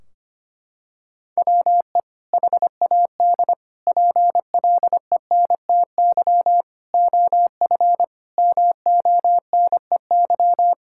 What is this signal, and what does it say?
Morse code, heard as a keyed tone, about 25 wpm, 700 Hz. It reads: WE HAD PLENTY OF MONEY
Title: Sentences from Top 1600 Words 25wpm